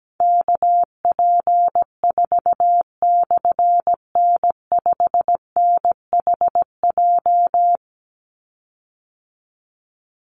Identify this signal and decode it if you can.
Morse code, heard as a keyed tone, about 17 wpm, 700 Hz. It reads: KP4/N5NHJ
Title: Call Signs - 17wpm